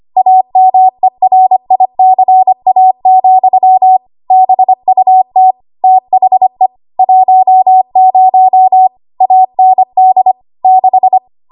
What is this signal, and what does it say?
Morse code, heard as a keyed tone, about 25 wpm, 750 Hz. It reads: AMERICA, BUT THE 10 AND 6
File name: Archive / 25 WPM / 25WPM